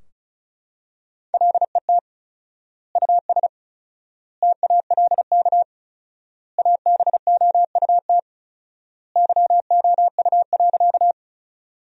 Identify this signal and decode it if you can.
Morse code, heard as a keyed tone, about 36 wpm, 700 Hz. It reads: LET US TALK ABOUT YOU.
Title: 4x Word-Spacing - Sentences from Top 400 Words - 35wpm